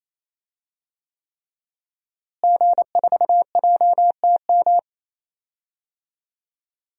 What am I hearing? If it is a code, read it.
Morse code, heard as a keyed tone, about 28 wpm, 700 Hz. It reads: G4JTM
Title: DXCC-100 Call Signs - 5 Characters - Repeated 3x - 28wpm